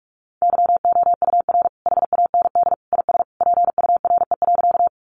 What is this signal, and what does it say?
Morse code, heard as a keyed tone, about 44 wpm, 700 Hz. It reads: YOUR HAND IS PURE.
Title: Sentences from Top 1500 Words 45wpm